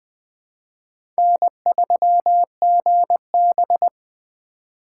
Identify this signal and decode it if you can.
Morse code, heard as a keyed tone, about 20 wpm, 700 Hz. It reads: N3GB
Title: USA Call Signs - 4 Characters - 4x Character Spacing - 20wpm